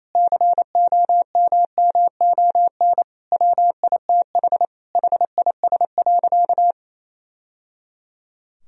Morse code, heard as a keyed tone, about 28 wpm, 700 Hz. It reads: COMMON WITH HIS.
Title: Sentences from Top 1000 Words - Review Part 3 - 28wpm